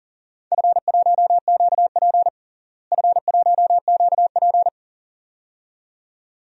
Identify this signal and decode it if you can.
Morse code, heard as a keyed tone, about 40 wpm, 700 Hz. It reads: F1QP F1QP
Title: DXCC-100 Call Signs - 4 Characters - Repeated 2x - 40wpm